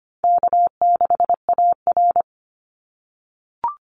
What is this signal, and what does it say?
Morse code, heard as a keyed tone, about 25 wpm, 700 Hz. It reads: K6AR
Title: Call Signs - 25wpm